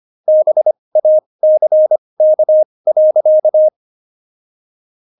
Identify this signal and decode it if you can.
Morse code, heard as a keyed tone, about 25 wpm, spaced as for 20 wpm, 620 Hz. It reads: BACK.
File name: Jokes-25-char-20-eff